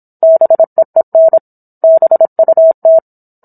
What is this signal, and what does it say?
Morse code, heard as a keyed tone, about 27 wpm, 650 Hz. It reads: BEEN BUT
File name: Book One / 17-Chapter 17 - 26 WPM